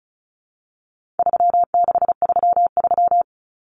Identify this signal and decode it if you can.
Morse code, heard as a keyed tone, about 34 wpm, 700 Hz. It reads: 3633
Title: Common Cut Numbers - 4-Digit 35wpm